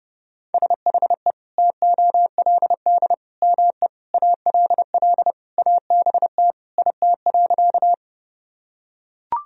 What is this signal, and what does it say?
Morse code, heard as a keyed tone, about 32 wpm, 700 Hz. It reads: SHE TOLD ME ALL ABT IT.
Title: Sentences from Top 400 Words and Abbreviations 30wpm